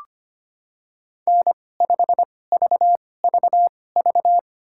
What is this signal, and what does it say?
Morse code, heard as a keyed tone, about 25 wpm, spaced as for 18 wpm, 700 Hz. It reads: N5VVV
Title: USA Call Signs - 5 Characters - 2x Character Spacing - 25wpm